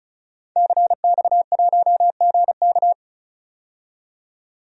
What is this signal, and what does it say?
Morse code, heard as a keyed tone, about 36 wpm, 700 Hz. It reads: CX1GK
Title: DXCC-100 Call Signs - 5 Characters - 35wpm